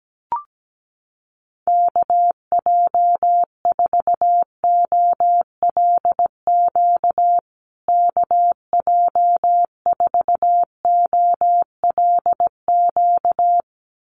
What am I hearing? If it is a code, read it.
Morse code, heard as a keyed tone, about 17 wpm, 700 Hz. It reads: KJ4OLQ KJ4OLQ
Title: USA Call Signs Repeated 2x 17wpm